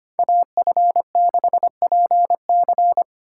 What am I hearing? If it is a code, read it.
Morse code, heard as a keyed tone, about 33 wpm, 700 Hz. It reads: AF6PC